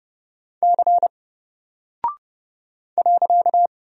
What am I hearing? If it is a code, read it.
Morse code, heard as a keyed tone, about 30 wpm, 700 Hz. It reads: C .
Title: Single Character (Letter:Number:Punctuation:Prosign) Alt 30wpm